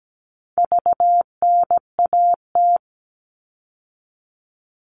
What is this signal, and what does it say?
Morse code, heard as a keyed tone, about 17 wpm, 700 Hz. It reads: VNAT
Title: All Cut Numbers - 4-Digit 17wpm